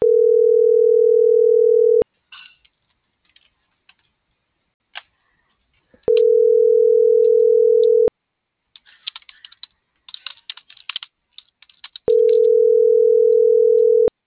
Ringback tone (North America), 440 + 480 Hz.